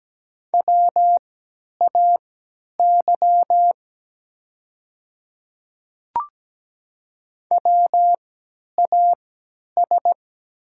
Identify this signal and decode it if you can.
Morse code, heard as a keyed tone, about 17 wpm, spaced as for 10 wpm, 700 Hz. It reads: WAY WAS
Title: 3x Character Spacing - Top 100 Words - 17wpm